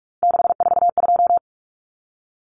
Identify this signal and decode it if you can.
Morse code, heard as a keyed tone, about 44 wpm, 700 Hz. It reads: 642
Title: Numbers - 3-Digit 45wpm